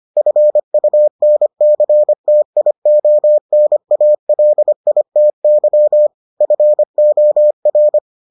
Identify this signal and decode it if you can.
Morse code, heard as a keyed tone, about 25 wpm, 600 Hz. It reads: FUNCTIONALITY FOR